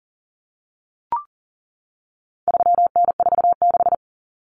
Morse code, heard as a keyed tone, about 38 wpm, 700 Hz. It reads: 3N46
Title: Common Cut Numbers - 4-Digit 40wpm